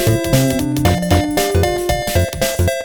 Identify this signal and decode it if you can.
There is no code.